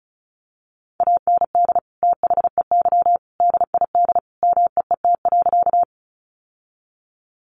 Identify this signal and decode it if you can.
Morse code, heard as a keyed tone, about 34 wpm, 700 Hz. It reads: AND THEY DID MEET.
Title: Sentences from Top 1100 Words 35wpm